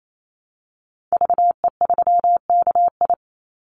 Morse code, heard as a keyed tone, about 28 wpm, 700 Hz. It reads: VE3KI
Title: Call Signs - 28wpm